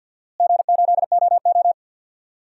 Morse code, heard as a keyed tone, about 51 wpm, 700 Hz. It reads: G8OO
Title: DXCC-100 Call Signs - 4 Characters - Repeated 2x - 50wpm